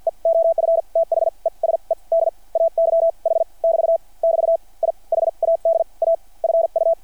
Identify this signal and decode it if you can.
Morse code, heard as a keyed tone, about 50 wpm, spaced as for 38 wpm, 660 Hz. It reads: E OF THESE DAY H <BT> <BT> I HAD A FU
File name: archived / CRSnet